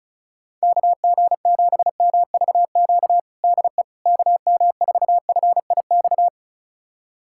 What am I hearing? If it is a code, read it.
Morse code, heard as a keyed tone, about 36 wpm, 700 Hz. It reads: KG7MVQ DE KM4FIX
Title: USA Callsign DE Callsign 35wpm